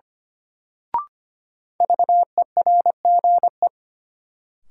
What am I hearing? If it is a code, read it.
Morse code, heard as a keyed tone, about 25 wpm, 700 Hz. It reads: VERGE
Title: Sets of 4 Words - New Word Drill - 25wpm